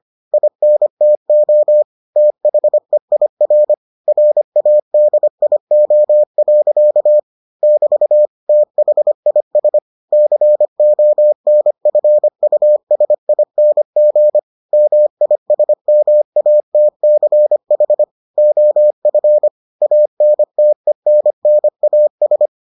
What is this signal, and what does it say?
Morse code, heard as a keyed tone, about 25 wpm, 600 Hz. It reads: INTO THEIR RADIO. <BT> THIS CONFUSING MISMATCH OF ANTENNAS